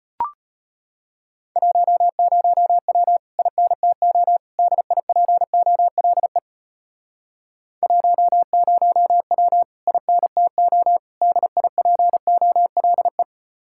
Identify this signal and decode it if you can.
Morse code, heard as a keyed tone, about 33 wpm, 700 Hz. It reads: 10W INTO DIPOLE 10W INTO DIPOLE